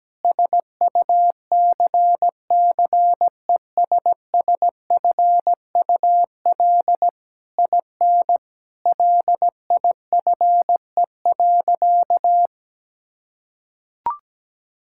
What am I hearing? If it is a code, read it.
Morse code, heard as a keyed tone, about 17 wpm, 700 Hz. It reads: SUCCESSFUL IN LIFE.
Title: Sentences from Top 1000 Words - Review Part 6 - 17wpm